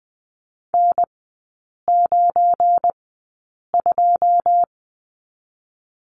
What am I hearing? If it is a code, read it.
Morse code, heard as a keyed tone, about 20 wpm, spaced as for 8 wpm, 700 Hz. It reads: N92
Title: 178 Lesson 29e - Slash - Three Characters 20wpm